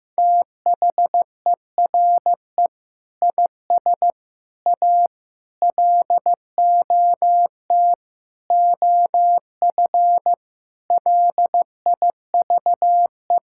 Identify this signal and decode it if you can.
Morse code, heard as a keyed tone, about 15 wpm, 700 Hz. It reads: THERE IS A LOT OF LIVE
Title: Sentences from Top 1000 Words - Review Part 5 - 15wpm